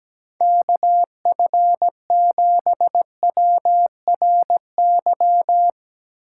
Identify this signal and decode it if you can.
Morse code, heard as a keyed tone, about 17 wpm, 700 Hz. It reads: KF7WRY
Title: USA Call Signs - 6 Characters - 3x Character Spacing - 17wpm